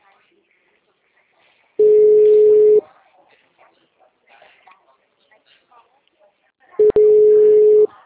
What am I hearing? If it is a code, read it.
Ringback tone (Europe, 425 Hz), with speech.